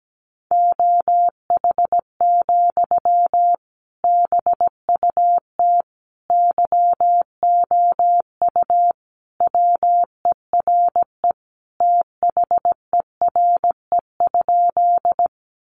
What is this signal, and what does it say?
Morse code, heard as a keyed tone, about 17 wpm, 700 Hz. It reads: OH, BUT YOU WERE THERE?
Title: Sentences from Top 500 Words 17wpm